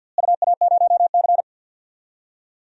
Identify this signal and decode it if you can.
Morse code, heard as a keyed tone, about 50 wpm, 700 Hz. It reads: UA0C
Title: DXCC-100 Call Signs - 4 Characters - Repeated 2x - 50wpm